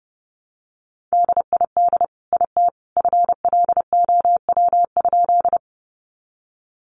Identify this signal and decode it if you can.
Morse code, heard as a keyed tone, about 30 wpm, 700 Hz. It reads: DID IT FLOW?
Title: Sentences from Top 1200 Words 30wpm